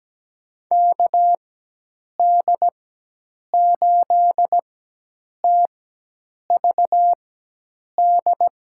Morse code, heard as a keyed tone, about 17 wpm, spaced as for 8 wpm, 700 Hz. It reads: KD8TVD
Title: USA Call Signs - 6 Characters - 4x Character Spacing - 17wpm